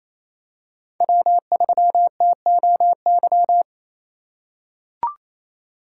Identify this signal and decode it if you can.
Morse code, heard as a keyed tone, about 28 wpm, 700 Hz. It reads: W3TOY
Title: USA Call Signs - 4x Character Spacing - 28wpm